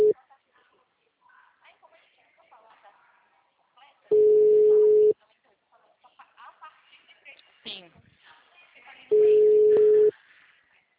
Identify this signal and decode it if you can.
Ringback tone (Europe, 425 Hz), with speech.
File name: out- -2022- - - .71111